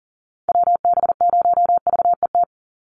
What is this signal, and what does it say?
Morse code, heard as a keyed tone, about 40 wpm, 700 Hz. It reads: WB0VET
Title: USA Call Signs 40wpm